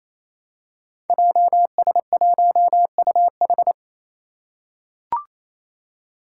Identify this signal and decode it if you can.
Morse code, heard as a keyed tone, about 28 wpm, 700 Hz. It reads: JS1UH
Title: Five Challenging Letter-Number Combinations 28wpm